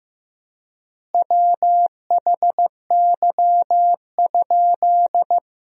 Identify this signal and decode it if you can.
Morse code, heard as a keyed tone, about 16 wpm, 700 Hz. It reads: WHY?
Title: 3x Word-Spacing - Sentences from Top 200 Words - 15wpm